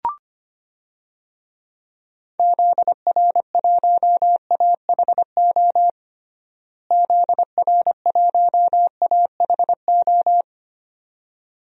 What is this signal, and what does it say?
Morse code, heard as a keyed tone, about 25 wpm, 700 Hz. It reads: ZR1AHO ZR1AHO
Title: DXCC-100 Call Signs - 6 Characters - Repeated 2x - 25wpm